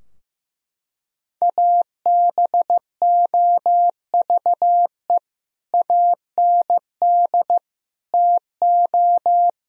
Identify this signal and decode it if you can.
Morse code, heard as a keyed tone, about 15 wpm, 700 Hz. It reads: ABOVE AND TO
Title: Sentences from Top 1000 Words - Review Part 2 - 15wpm